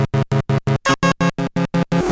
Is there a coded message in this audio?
No.